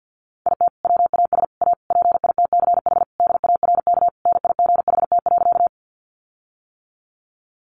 Morse code, heard as a keyed tone, about 48 wpm, 700 Hz. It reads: IT WAS A PITCH DARK NIGHT.
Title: Sentences from Top 1300 Words 50wpm